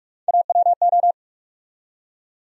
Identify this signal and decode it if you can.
Morse code, heard as a keyed tone, about 47 wpm, 700 Hz. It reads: AWO
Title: USA Call Sign Postfixes - 3 Letters 45wpm